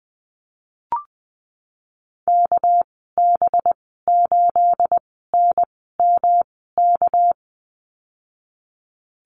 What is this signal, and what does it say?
Morse code, heard as a keyed tone, about 20 wpm, spaced as for 15 wpm, 700 Hz. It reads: KB8NMK
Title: 245 Lesson 39g - Letter z - Call Signs 20wpm